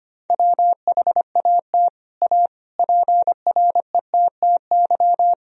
Morse code, heard as a keyed tone, about 25 wpm, 700 Hz. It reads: WHAT A PRETTY
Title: Sentences from Top 1000 Words - Review Part 2 - 25wpm